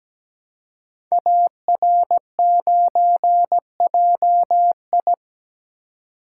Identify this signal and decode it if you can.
Morse code, heard as a keyed tone, about 17 wpm, 700 Hz. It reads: AR9JI
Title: Code Groups - 5 Letter:Numbers - 17wpm